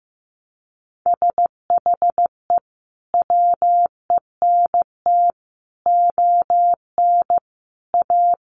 Morse code, heard as a keyed tone, about 15 wpm, 700 Hz. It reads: SHE WENT ON A
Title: Sentences from Top 1400 Words 15wpm